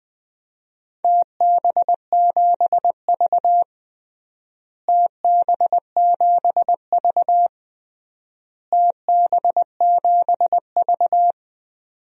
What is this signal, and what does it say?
Morse code, heard as a keyed tone, about 19 wpm, 700 Hz. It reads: TB7V TB7V TB7V
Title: DXCC-100 Call Signs - 4 Characters - Repeated 3x - 20wpm